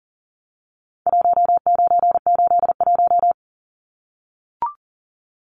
Morse code, heard as a keyed tone, about 40 wpm, 700 Hz. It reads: 1981
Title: Numbers - Contest Style - 4-Digit 40wpm